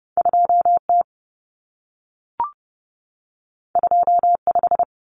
Morse code, heard as a keyed tone, about 30 wpm, 700 Hz. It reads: 2T 25
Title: Common Cut Numbers - 2-Digit 30wpm